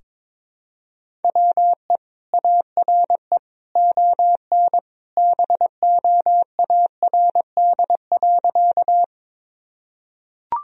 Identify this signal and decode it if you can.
Morse code, heard as a keyed tone, about 22 wpm, 700 Hz. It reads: WE ARE ON BOARD.
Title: Sentences from Top 1000 Words - Review Part 4 - 22wpm